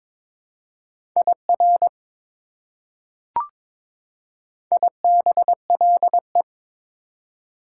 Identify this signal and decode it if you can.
Morse code, heard as a keyed tone, about 23 wpm, 700 Hz. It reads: IR IBLE
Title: Prefixes and Suffixes 22wpm